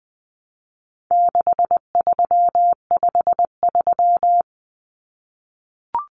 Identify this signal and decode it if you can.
Morse code, heard as a keyed tone, about 20 wpm, 700 Hz. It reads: 6353
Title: Numbers - 4-Digit 20wpm